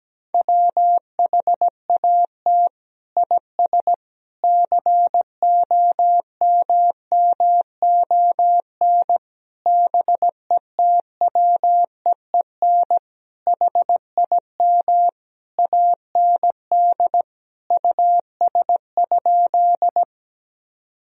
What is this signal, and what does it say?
Morse code, heard as a keyed tone, about 17 wpm, 700 Hz. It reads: WHAT IS COMMON BETWEEN HIM AND US?
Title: Sentences from Top 1000 Words - Review Part 3 - 17wpm